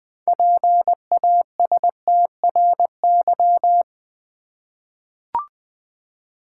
Morse code, heard as a keyed tone, about 20 wpm, 700 Hz. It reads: PASTRY
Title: Easy Words with Hints - 20wpm